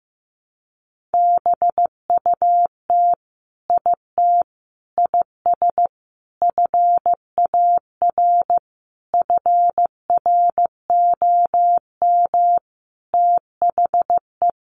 Morse code, heard as a keyed tone, about 15 wpm, 700 Hz. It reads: BUT IT IS FAR FROM THE
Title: Sentences from Top 1400 Words 15wpm